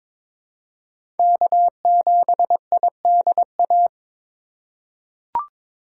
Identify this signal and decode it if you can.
Morse code, heard as a keyed tone, about 22 wpm, 700 Hz. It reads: K7IDA
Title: USA Call Signs - 5 Characters - 4x Character Spacing - 22wpm